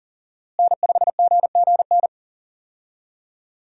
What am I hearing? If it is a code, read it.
Morse code, heard as a keyed tone, about 40 wpm, 700 Hz. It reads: N5GGN
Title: USA Call Signs - 5 Characters - Repeated 3x - 40wpm